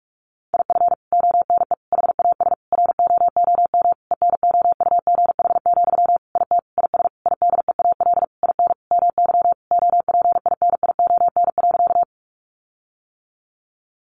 Morse code, heard as a keyed tone, about 44 wpm, 700 Hz. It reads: IF ONE HAS ROOM ENOUGH, IT IS IDEAL IN MY OPINION.
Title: Sentences from Top 1400 Words 45wpm